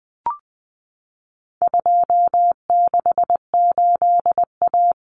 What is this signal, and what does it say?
Morse code, heard as a keyed tone, about 20 wpm, 700 Hz. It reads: 268A
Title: Common Cut Numbers - 4-Digit 20wpm